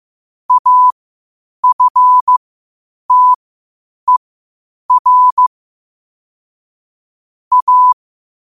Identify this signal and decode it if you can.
Morse code, heard as a keyed tone, about 15 wpm, spaced as for 8 wpm, 1000 Hz. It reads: AFTER A
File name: qotdfiles / 8wpm